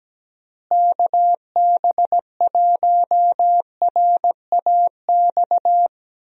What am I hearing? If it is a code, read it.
Morse code, heard as a keyed tone, about 17 wpm, 700 Hz. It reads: KB1RAX
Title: USA Call Signs - 6 Characters - Repeated 2x - 17wpm